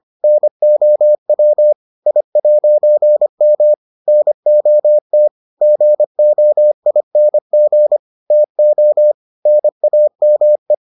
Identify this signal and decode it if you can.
Morse code, heard as a keyed tone, about 25 wpm, 600 Hz. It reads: NOW I'M NOT GOING TO NAME